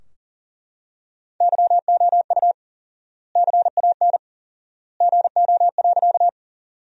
Morse code, heard as a keyed tone, about 40 wpm, 700 Hz. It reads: YOU CAN GO.
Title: 4x Word-Spacing - Sentences from Top 100 Words - 40wpm